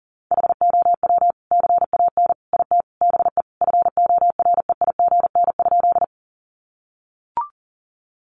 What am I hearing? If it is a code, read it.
Morse code, heard as a keyed tone, about 40 wpm, 700 Hz. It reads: HOW CAN IT BE FOREIGN?
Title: Sentences from Top 1400 Words 40wpm